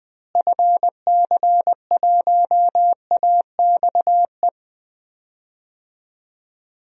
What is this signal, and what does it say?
Morse code, heard as a keyed tone, about 20 wpm, 700 Hz. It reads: FC1AXE
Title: DXCC-100 Call Signs - 6 Characters - Repeated 3x - 20wpm